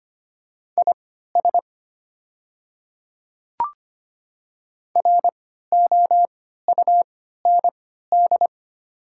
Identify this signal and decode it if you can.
Morse code, heard as a keyed tone, about 25 wpm, spaced as for 14 wpm, 700 Hz. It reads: IS ROUND
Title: 3x Character Spacing - Top 300 Words - 25wpm